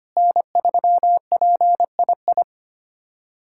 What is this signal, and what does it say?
Morse code, heard as a keyed tone, about 25 wpm, 700 Hz. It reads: N3PII
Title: USA Callsign DE Callsign - 5 Characters - 25wpm